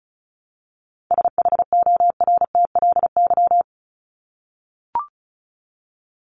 Morse code, heard as a keyed tone, about 34 wpm, 700 Hz. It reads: SHORTLY
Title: LY Suffix - Top 100 Words with Base Words 35wpm